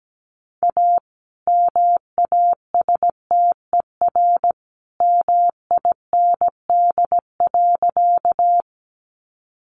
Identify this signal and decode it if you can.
Morse code, heard as a keyed tone, about 17 wpm, 700 Hz. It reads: A MASTER MIND.
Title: Sentences from Top 1300 Words 17wpm